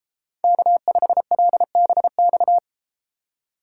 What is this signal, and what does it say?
Morse code, heard as a keyed tone, about 33 wpm, 700 Hz. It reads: K5LBX